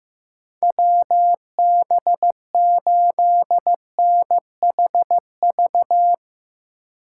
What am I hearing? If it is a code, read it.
Morse code, heard as a keyed tone, about 15 wpm, 700 Hz. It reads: WB8NHV
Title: USA Call Signs - 3x Character Spacing - 15wpm